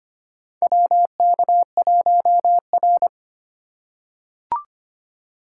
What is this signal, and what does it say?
Morse code, heard as a keyed tone, about 25 wpm, 700 Hz. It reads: WK1R
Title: USA Call Signs - 4 Characters - 2x Character Spacing - 25wpm